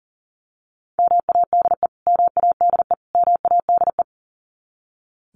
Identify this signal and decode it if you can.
Morse code, heard as a keyed tone, about 40 wpm, 700 Hz. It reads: MADE MADE MADE
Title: Top 200 Words Repeated 40wpm